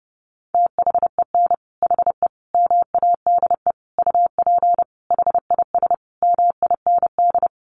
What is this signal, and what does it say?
Morse code, heard as a keyed tone, about 30 wpm, 700 Hz. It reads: THEN HE MADE UP HIS MIND
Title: Sentences from Top 1400 Words 30wpm